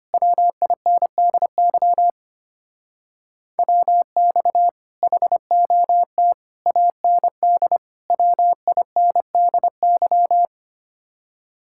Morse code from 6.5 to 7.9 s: AND

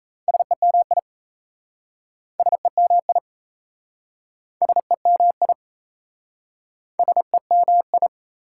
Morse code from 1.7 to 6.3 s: SEMI SEMI

